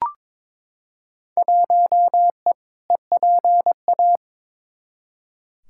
Morse code, heard as a keyed tone, about 21 wpm, 700 Hz. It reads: 1E EPA
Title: Field Day Exchanges - 22wpm